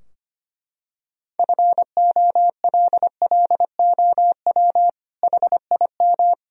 Morse code, heard as a keyed tone, about 25 wpm, 700 Hz. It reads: FOLLOW HIM
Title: Sentences from Top 1000 Words - Review Part 3 - 25wpm